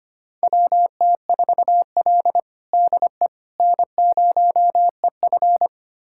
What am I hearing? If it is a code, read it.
Morse code, heard as a keyed tone, about 25 wpm, 700 Hz. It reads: WT4L DE N0EF
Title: USA Callsign DE Callsign - 4 Characters - 25wpm